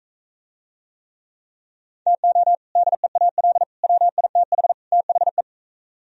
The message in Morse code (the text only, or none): TO DEAL WITH THE